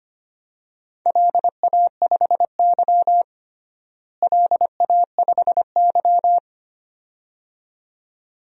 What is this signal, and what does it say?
Morse code, heard as a keyed tone, about 25 wpm, 700 Hz. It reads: LA5Y LA5Y
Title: DXCC-100 Call Signs - 4 Characters - Repeated 2x - 25wpm